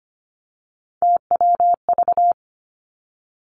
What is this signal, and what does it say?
Morse code, heard as a keyed tone, about 25 wpm, 700 Hz. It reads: TWV
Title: Three Challenging Letter-Number Combinations 25wpm